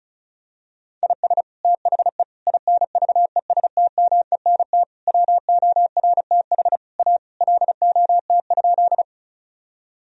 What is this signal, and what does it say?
Morse code, heard as a keyed tone, about 36 wpm, 700 Hz. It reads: IS THE INVESTMENT WORTH A LOT?
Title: Sentences from Top 1000 Words - Review Part 4 - 35wpm